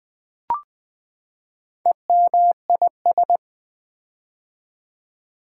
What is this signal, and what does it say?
Morse code, heard as a keyed tone, about 20 wpm, 700 Hz. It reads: EMIS
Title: Build A Word - 4 Letters At A Time - 8-Letter Words - 20wpm